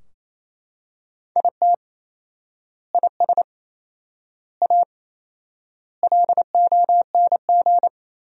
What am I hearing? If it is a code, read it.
Morse code, heard as a keyed tone, about 28 wpm, 700 Hz. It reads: IT IS A LONG
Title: 4x Word-Spacing - Sentences from Top 300 Words - 28wpm